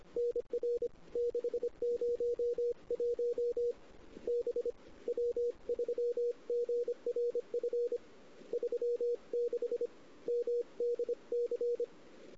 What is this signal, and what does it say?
Morse code, heard as a keyed tone, about 26 wpm, 440 Hz. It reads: NR 601 B W3GRF 36 MDC
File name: CW / 2008-ss-cw-k5tr-op-n5rz -06:00